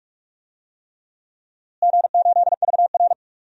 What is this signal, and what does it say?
Morse code, heard as a keyed tone, about 41 wpm, 700 Hz. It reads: G8VR
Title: DXCC-100 Call Signs - 4 Characters - Repeated 3x - 45wpm